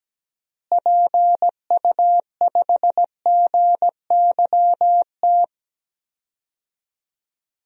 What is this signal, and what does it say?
Morse code, heard as a keyed tone, about 17 wpm, 700 Hz. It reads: PU5GYT
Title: DXCC-100 Call Signs - 6 Characters - Repeated 2x - 17wpm